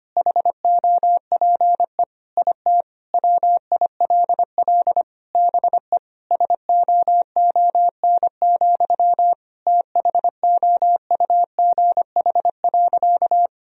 Morse code, heard as a keyed tone, about 25 wpm, 700 Hz. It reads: HOPE IT WILL BE SOON, THOUGH.
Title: Sentences from Top 1000 Words - Review Part 2 - 25wpm